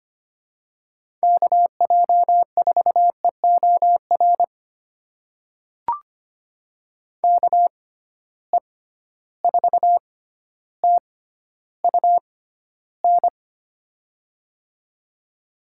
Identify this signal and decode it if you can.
Morse code, heard as a keyed tone, about 25 wpm, spaced as for 13 wpm, 700 Hz. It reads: KJ4EOR KE4TUN
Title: USA Call Signs - 6 Characters - 6x Character Spacing - 25wpm